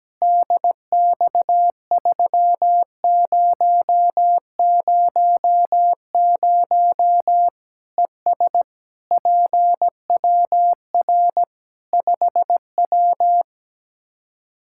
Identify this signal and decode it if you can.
Morse code, heard as a keyed tone, about 17 wpm, 700 Hz. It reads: DX3000 ES PWR 5W
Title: K1USN SST QSO Practice 17wpm